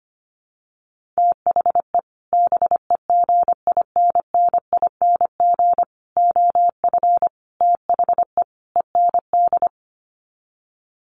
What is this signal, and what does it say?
Morse code, heard as a keyed tone, about 25 wpm, 700 Hz. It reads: THE BEGINNING OF THE END
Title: Sets of 5 Words Encore2 25wpm